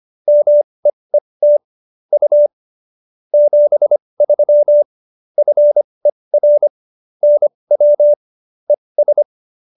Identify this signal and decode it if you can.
Morse code, heard as a keyed tone, about 25 wpm, spaced as for 20 wpm, 590 Hz. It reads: MEET U 73 FER NW ES